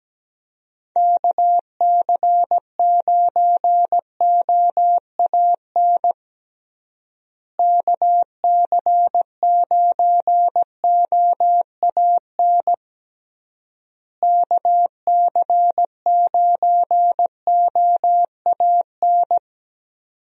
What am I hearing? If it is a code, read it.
Morse code, heard as a keyed tone, about 17 wpm, 700 Hz. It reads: KC9OAN KC9OAN KC9OAN
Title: USA Call Signs - 6 Characters - Repeated 3x - 17wpm